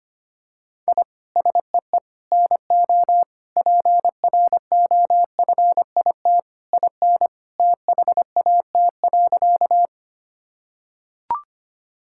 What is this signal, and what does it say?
Morse code, heard as a keyed tone, about 25 wpm, 700 Hz. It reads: I SEE NO PROFIT IN THAT.
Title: Sentences from Top 1000 Words - Review Part 5 - 25wpm